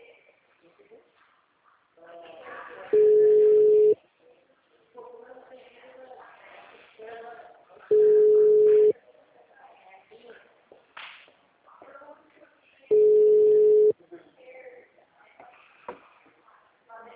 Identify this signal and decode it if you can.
Ringback tone (Europe, 425 Hz), with speech.
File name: out- -2010- - - .21